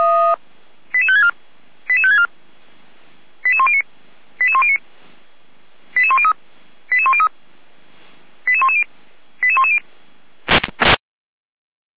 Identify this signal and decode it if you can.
Five-tone selective call (ZVEI), heard as quick runs of five tones, tones of about 70 ms. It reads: 88564 88564 88109 88109 88103 88103 88110 88110